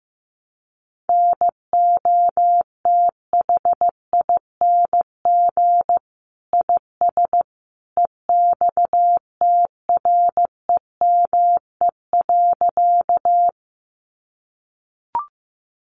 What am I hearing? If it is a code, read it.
Morse code, heard as a keyed tone, about 14 wpm, 700 Hz. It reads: NOTHING IS EXTREME.
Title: Sentences from Top 1300 Words 15wpm